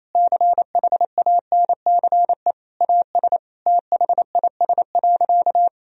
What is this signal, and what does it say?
Morse code, heard as a keyed tone, about 28 wpm, 700 Hz. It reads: CHANCE AS THIS.
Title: Sentences from Top 1000 Words - Review Part 1 - 28wpm